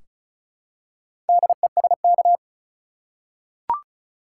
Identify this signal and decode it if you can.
Morse code, heard as a keyed tone, about 36 wpm, 700 Hz. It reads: DESK
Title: Four Letter Words - 35wpm